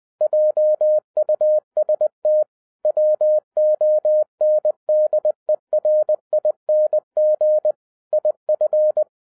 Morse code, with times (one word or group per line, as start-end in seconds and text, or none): 0.2-2.4 JUST
2.9-7.7 WONDERING
8.1-9.0 IF